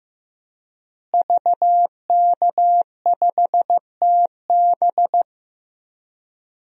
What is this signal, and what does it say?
Morse code, heard as a keyed tone, about 15 wpm, 700 Hz. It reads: VK5TB
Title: DXCC-100 Call Signs - 5 Characters - Repeated 3x - 15wpm